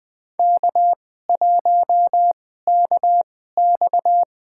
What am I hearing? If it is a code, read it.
Morse code, heard as a keyed tone, about 21 wpm, spaced as for 15 wpm, 700 Hz. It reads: K1KX
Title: USA Call Signs - 4 Characters - 2x Character Spacing - 20wpm